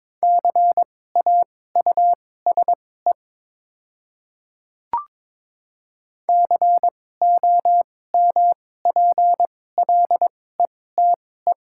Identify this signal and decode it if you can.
Morse code, heard as a keyed tone, about 22 wpm, spaced as for 16 wpm, 700 Hz. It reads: CAUSE COMPLETE
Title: 2x Character Spacing - Top 500 Words - 22wpm